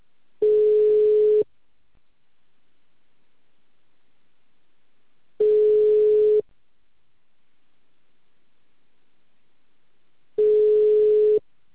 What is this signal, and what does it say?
Ringback tone (Europe, 425 Hz).